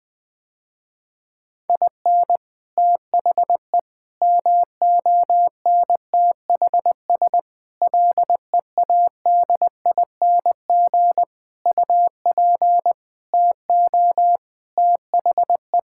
Morse code, heard as a keyed tone, about 20 wpm, 700 Hz. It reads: IN THE MONTHS LEADING UP TO THE
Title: Sentences with 6-Word Phrases - Spoken Beginning - 20wpm